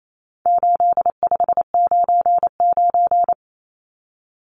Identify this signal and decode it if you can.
Morse code, heard as a keyed tone, about 28 wpm, 700 Hz. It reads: 8599
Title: Numbers - 4-Digit 28wpm